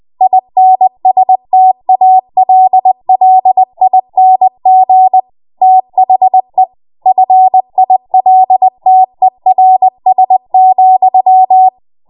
Morse code, heard as a keyed tone, about 20 wpm, 750 Hz. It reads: INSTALLING THE FILTERS,
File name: Morse / 20WPM